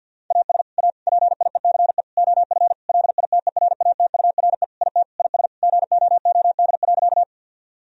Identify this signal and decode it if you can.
Morse code, heard as a keyed tone, about 48 wpm, 700 Hz. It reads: AS A PIECE OF LITERATURE IT IS GOOD.
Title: Sentences from Top 1000 Words - Review Part 3 - 50wpm